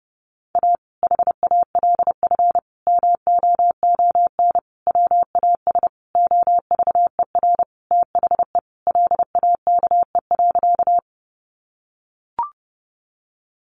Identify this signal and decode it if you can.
Morse code, heard as a keyed tone, about 30 wpm, 700 Hz. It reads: A HALF MOON WAS OVER THE LAKE.
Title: Sentences from Top 1100 Words 30wpm